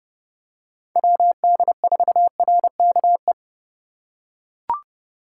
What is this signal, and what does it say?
Morse code, heard as a keyed tone, about 30 wpm, 700 Hz. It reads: WD4RKE
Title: USA Call Signs - 6 Characters - 4x Character Spacing - 30wpm